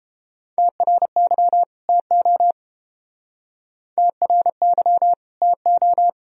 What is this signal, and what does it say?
Morse code, heard as a keyed tone, about 29 wpm, 700 Hz. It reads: TRY TO TRY TO